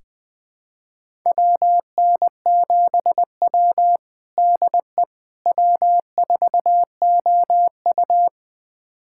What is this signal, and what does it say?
Morse code, heard as a keyed tone, about 20 wpm, 700 Hz. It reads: WN7W DE W4OU
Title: USA Callsign DE Callsign - 4 Characters - 20wpm